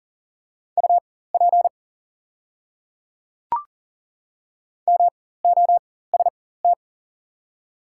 Morse code, heard as a keyed tone, about 40 wpm, spaced as for 19 wpm, 700 Hz. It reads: UP MOST
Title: 4x Character Spacing - Top 100 Words - 40wpm